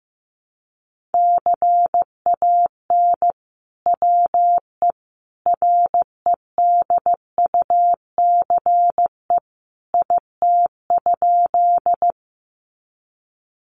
Morse code, heard as a keyed tone, about 15 wpm, 700 Hz. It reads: CAN WE REDUCE IT?
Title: Sentences from Top 1100 Words 15wpm